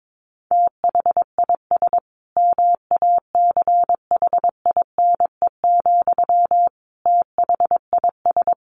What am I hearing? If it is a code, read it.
Morse code, heard as a keyed tone, about 22 wpm, 700 Hz. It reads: THIS MACHINE, THIS
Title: A Sound Of Thunder 22wpm